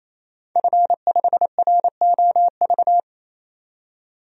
Morse code, heard as a keyed tone, about 28 wpm, 700 Hz. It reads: F5ROV
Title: DXCC-100 Call Signs - 5 Characters - 28wpm